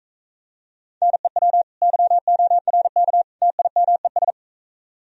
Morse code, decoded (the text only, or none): NEW YORK TIMES